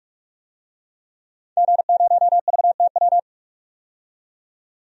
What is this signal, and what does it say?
Morse code, heard as a keyed tone, about 46 wpm, 700 Hz. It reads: G0VTW
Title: DXCC-100 Call Signs - 5 Characters - 45wpm